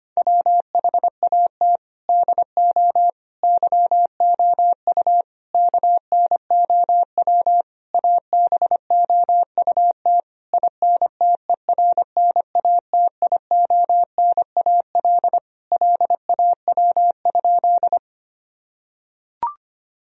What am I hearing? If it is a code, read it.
Morse code, heard as a keyed tone, about 25 wpm, 700 Hz. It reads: WHAT DO YOU KNOW ABOUT INTERNATIONAL LAW?
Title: Sentences from Top 1000 Words - Review Part 4 - 25wpm